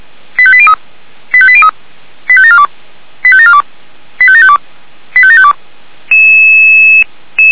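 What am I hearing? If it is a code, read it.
Five-tone selective call (ZVEI), heard as quick runs of five tones, tones of about 70 ms. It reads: 85803 85803 85742 85742 85752 85752
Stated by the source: ZVEI five-tone call 85753 received on SDR, logged by the Feuerwehr Orlamünde (f85360) server.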